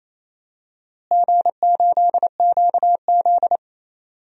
Morse code, heard as a keyed tone, about 28 wpm, 700 Hz. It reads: G8QZ
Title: DXCC-100 Call Signs - 4 Characters - Repeated 2x - 28wpm